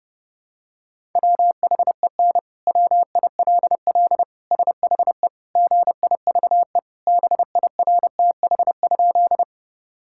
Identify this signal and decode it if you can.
Morse code, heard as a keyed tone, about 30 wpm, 700 Hz. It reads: WHEN WILL SHE GIVE BIRTH?
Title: Sentences from Top 1000 Words - Review Part 5 - 30wpm